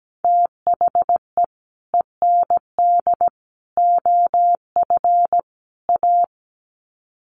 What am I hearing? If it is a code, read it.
Morse code, heard as a keyed tone, about 17 wpm, 700 Hz. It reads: THE END OF A
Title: Sets of 4 Words 17wpm